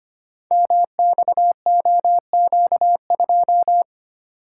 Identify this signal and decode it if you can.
Morse code, heard as a keyed tone, about 25 wpm, 700 Hz. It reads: MXOQ2
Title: Code Groups - 5 Letter:Numbers - 25wpm